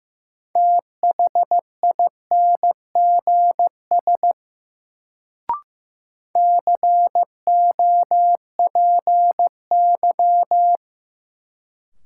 Morse code, heard as a keyed tone, about 15 wpm, 700 Hz. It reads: THINGS COPY
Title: Sentences from Top 1400 Words - New Word Drill Alt - 15wpm